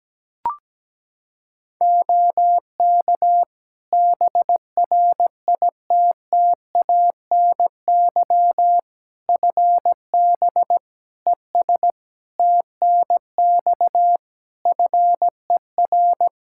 Morse code, heard as a keyed tone, about 17 wpm, 700 Hz. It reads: OK BRITTANY FB ES TNX FER
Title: Practice QSO - Exchange 2 - Info - 17wpm